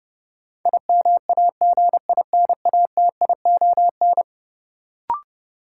Morse code, heard as a keyed tone, about 30 wpm, 700 Hz. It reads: IMAGINATION
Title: Sentences from Top 1300 Words - New Word Drill Alt - 30wpm